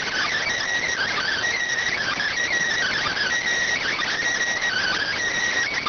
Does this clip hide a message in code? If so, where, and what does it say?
Yes: Emergency Alert System (SAME) tones (header bursts) from 0.9 to 5.2 s, heard as harsh buzzing bursts.